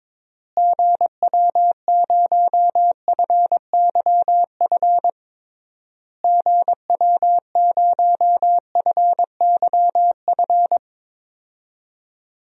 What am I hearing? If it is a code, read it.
Morse code, heard as a keyed tone, about 22 wpm, 700 Hz. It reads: GW0FYF GW0FYF
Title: DXCC-100 Call Signs - 6 Characters - Repeated 3x - 22wpm